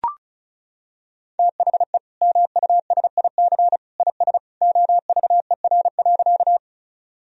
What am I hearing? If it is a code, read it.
Morse code, heard as a keyed tone, about 34 wpm, 700 Hz. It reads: THE MUSIC IS OVER.
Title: Sentences from Top 1000 Words - Review Part 1 - 35wpm